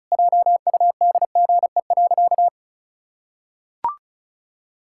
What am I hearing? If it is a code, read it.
Morse code, heard as a keyed tone, about 34 wpm, 700 Hz. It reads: JUDGE.
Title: Pangram Challenge v2 - 35wpm